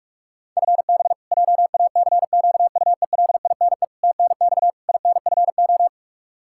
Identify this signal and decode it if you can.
Morse code, heard as a keyed tone, about 43 wpm, 700 Hz. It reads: FB JACQUELINE TNX INFO
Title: Practice QSO - Exchange 3 - More Info - 45wpm